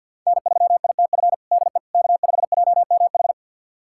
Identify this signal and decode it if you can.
Morse code, heard as a keyed tone, about 51 wpm, 690 Hz. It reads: N3ITF DE K5JMH
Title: USA Callsign DE Callsign - 5 Characters - 50wpm